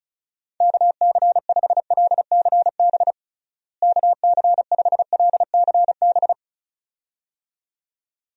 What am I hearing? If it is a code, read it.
Morse code, heard as a keyed tone, about 36 wpm, 700 Hz. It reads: KC5LCB KC5LCB
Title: USA Call Signs - 6 Characters - Repeated 2x - 35wpm